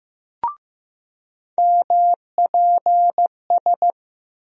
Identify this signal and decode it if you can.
Morse code, heard as a keyed tone, about 15 wpm, 700 Hz. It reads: MPS
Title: USA Call Sign Postfixes - 3 Letters 15wpm